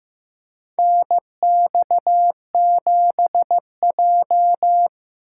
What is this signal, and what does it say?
Morse code, heard as a keyed tone, about 16 wpm, 700 Hz. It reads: NX7J
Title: USA Call Signs - 4 Characters - 4x Character Spacing - 15wpm